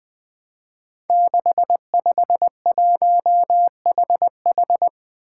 Morse code, heard as a keyed tone, about 20 wpm, 700 Hz. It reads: 651HH
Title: Five Challenging Letter-Number Combinations 20wpm